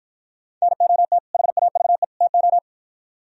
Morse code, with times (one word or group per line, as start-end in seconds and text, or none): 0.6-1.2 NOT
1.3-2.0 HAVE
2.2-2.6 TO